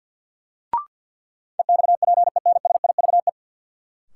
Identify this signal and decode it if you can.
Morse code, heard as a keyed tone, about 53 wpm, 700 Hz. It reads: EXPENSIVE
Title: Sentences from Top 1000 Words - Review Part 6 - New Word Drill - 50wpm